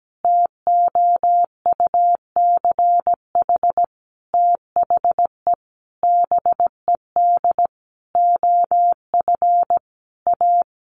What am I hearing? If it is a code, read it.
Morse code, heard as a keyed tone, about 17 wpm, 700 Hz. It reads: TOUCH THE BED OF A
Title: Sentences from Top 1200 Words 17wpm